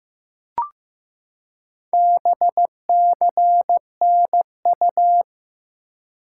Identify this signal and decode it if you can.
Morse code, heard as a keyed tone, about 15 wpm, 700 Hz. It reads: BCNU
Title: Top 90 QSO Elements 15wpm